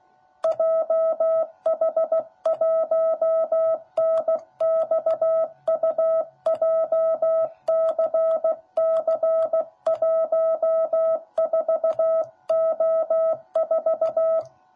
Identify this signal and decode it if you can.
Morse code, heard as a keyed tone, about 15 wpm, 660 Hz. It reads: JH1NXU JCC14O4
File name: jh8cbh / 2021cw / cw10b